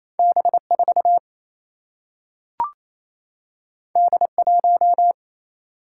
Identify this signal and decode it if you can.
Morse code, heard as a keyed tone, about 28 wpm, 700 Hz. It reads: B 4 D 1
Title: Single-Letter Single-Number 28wpm